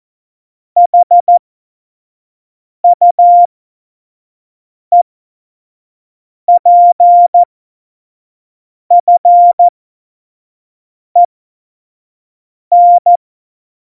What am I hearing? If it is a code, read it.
Morse code, heard as a keyed tone, about 14 wpm, spaced as for 5 wpm, 700 Hz. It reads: HUEPFEN
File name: Morsekurs / Witze / Witz 12 Mai 2020, 25 F70